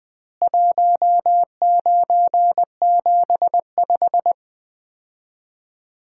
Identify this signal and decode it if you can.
Morse code, heard as a keyed tone, about 21 wpm, 700 Hz. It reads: 1975
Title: Common QSO Phrases - 2x Word-Spacing - 20wpm